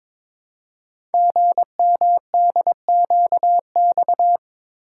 Morse code, heard as a keyed tone, about 23 wpm, 700 Hz. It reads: GMDQX
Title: Code Groups - 5 Letters - 22wpm